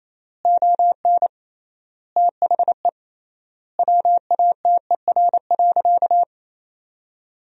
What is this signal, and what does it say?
Morse code, heard as a keyed tone, about 28 wpm, 700 Hz. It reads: ON THE WATER.
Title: 3x Word-Spacing - Sentences from Top 500 Words - 28wpm